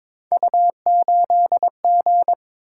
Morse code, heard as a keyed tone, about 22 wpm, 700 Hz. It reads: U8G
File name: practice / 22wpm / Three Letter-Number Combinations - 22wpm